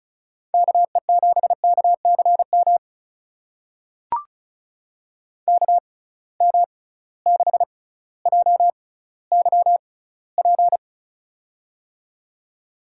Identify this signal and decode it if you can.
Morse code, heard as a keyed tone, about 34 wpm, spaced as for 18 wpm, 700 Hz. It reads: KE7KCM KM6JYP
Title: USA Call Signs - 6 Characters - 6x Character Spacing - 35wpm